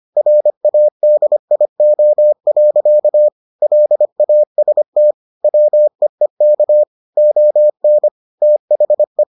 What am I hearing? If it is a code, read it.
Morse code, heard as a keyed tone, about 25 wpm, 600 Hz. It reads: RADIO. LAST WEEK ON THE